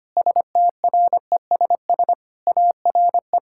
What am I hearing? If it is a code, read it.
Morse code, heard as a keyed tone, about 25 wpm, 700 Hz. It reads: STRESS ARE
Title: Sentences from Top 1000 Words - Review Part 6 - 25wpm